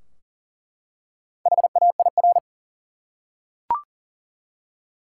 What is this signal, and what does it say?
Morse code, heard as a keyed tone, about 40 wpm, 700 Hz. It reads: HAIR
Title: Four Letter Words - 40wpm